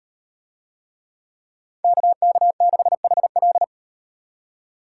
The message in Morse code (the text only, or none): KK6HL